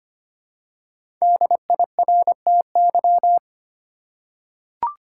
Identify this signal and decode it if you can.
Morse code, heard as a keyed tone, about 25 wpm, 700 Hz. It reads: DIRTY
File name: 5 Letter Words - First 3 Letters Given - 25wpm